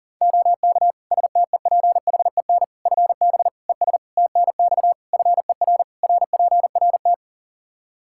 Morse code, heard as a keyed tone, about 40 wpm, 700 Hz. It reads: OK STEPHEN FB ES TNX FER RPRT
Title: K1USN SST QSO Practice 40wpm